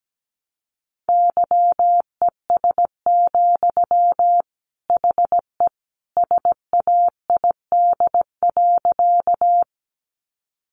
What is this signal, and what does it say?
Morse code, heard as a keyed tone, about 17 wpm, 700 Hz. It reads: YES, HE SAID.
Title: Sentences from Top 500 Words 17wpm